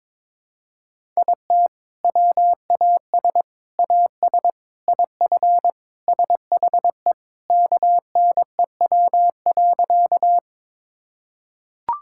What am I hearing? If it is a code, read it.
Morse code, heard as a keyed tone, about 22 wpm, 700 Hz. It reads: IT WAS AS IF SHE KNEW.
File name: practice / 22wpm / Sentences with 4-Word Phrases - 22wpm